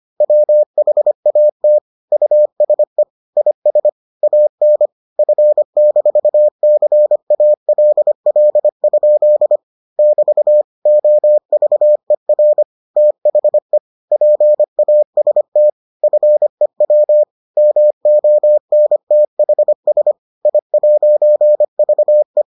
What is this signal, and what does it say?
Morse code, heard as a keyed tone, about 25 wpm, 600 Hz. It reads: WHAT USE IS AN F-CALL? <BT> OVER THE PAST FEW MONTHS I'VE